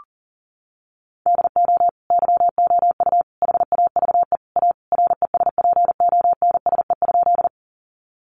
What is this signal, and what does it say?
Morse code, heard as a keyed tone, about 40 wpm, 700 Hz. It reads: DO YOU HAVE A RESPONSE?
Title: Sentences from Top 1400 Words 40wpm